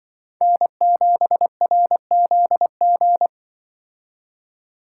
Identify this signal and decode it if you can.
Morse code, heard as a keyed tone, about 23 wpm, 700 Hz. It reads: N7RZG
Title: Speed Racing - USA 1x3 Call Signs 20wpm